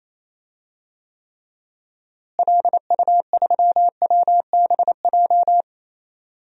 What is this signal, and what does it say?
Morse code, heard as a keyed tone, about 28 wpm, 700 Hz. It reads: LU3WBJ
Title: DXCC-100 Call Signs - 6 Characters - Repeated 3x - 28wpm